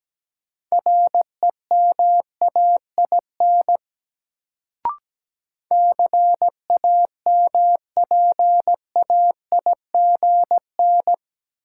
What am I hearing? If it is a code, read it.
Morse code, heard as a keyed tone, about 17 wpm, 700 Hz. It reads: REMAIN CAMPAIGN
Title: Sentences from Top 1200 Words - New Word Drill Alt - 17wpm